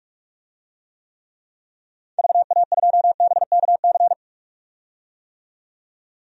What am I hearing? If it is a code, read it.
Morse code, heard as a keyed tone, about 44 wpm, 700 Hz. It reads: VA2BKC
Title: DXCC-100 Call Signs - 6 Characters - 45wpm